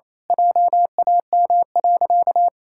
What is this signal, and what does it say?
Morse code, heard as a keyed tone, about 28 wpm, 700 Hz. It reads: JAM.
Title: Pangram Challenge v2 - 28wpm